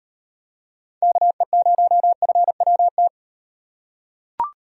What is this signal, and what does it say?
Morse code, heard as a keyed tone, about 38 wpm, 700 Hz. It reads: KE0FWT